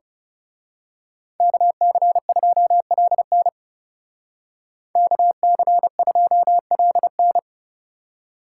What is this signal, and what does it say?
Morse code, heard as a keyed tone, about 34 wpm, 700 Hz. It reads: KC2LN KC2LN
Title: Speed Racing - USA 2x2 Call Signs 30wpm